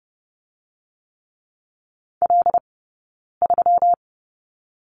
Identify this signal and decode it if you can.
Morse code, heard as a keyed tone, about 30 wpm, 700 Hz. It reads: L 3
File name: Lesson 20d - Number 3 - Two Characters 30wpm